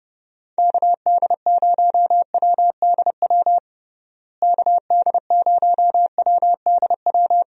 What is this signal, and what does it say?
Morse code, heard as a keyed tone, about 32 wpm, 700 Hz. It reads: KD0WDW KD0WDW
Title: USA Call Signs - 6 Characters - Repeated 3x - 30wpm